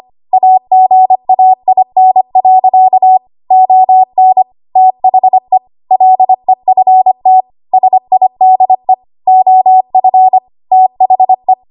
Morse code, heard as a keyed tone, about 25 wpm, 750 Hz. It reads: AGAIN. ON THE LEFT SIDE OF THE
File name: Archive / 25 WPM / 25WPM